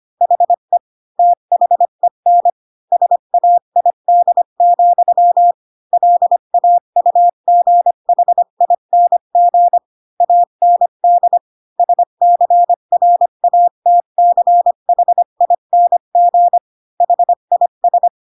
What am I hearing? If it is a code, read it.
Morse code, heard as a keyed tone, about 25 wpm, 700 Hz. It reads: HE THEN SAID, LAUGHING AND SCRATCHING HIS
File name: Pinocchio / Pinocchio ch1 20 25 ik1hge